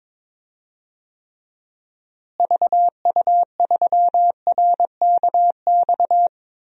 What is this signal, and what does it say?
Morse code, heard as a keyed tone, about 22 wpm, 700 Hz. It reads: VU3RKX
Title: DXCC-100 Call Signs - 6 Characters - Repeated 3x - 22wpm